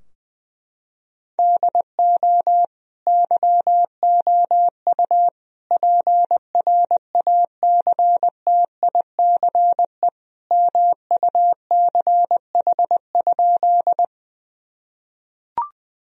Morse code, heard as a keyed tone, about 20 wpm, 700 Hz. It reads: DO YOU PRACTICE MUCH?
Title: Sentences from Top 1000 Words - Review Part 4 - 20wpm